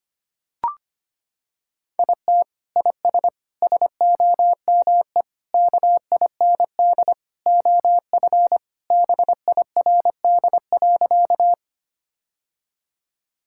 Morse code, heard as a keyed tone, about 25 wpm, 700 Hz. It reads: IT IS SOME KIND OF BIRD.
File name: practice / 25wpm / Sentences with 3-Word Phrases - 25wpm